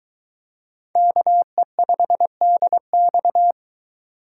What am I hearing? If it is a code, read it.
Morse code, heard as a keyed tone, about 23 wpm, 700 Hz. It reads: KE5DX